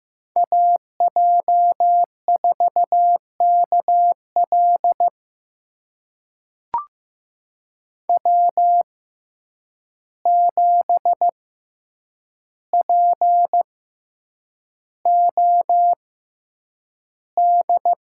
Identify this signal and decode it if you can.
Morse code, heard as a keyed tone, about 15 wpm, spaced as for 8 wpm, 700 Hz. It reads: AJ4KL W7POD
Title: USA Call Signs - 5 Characters - 6x Character Spacing - 15wpm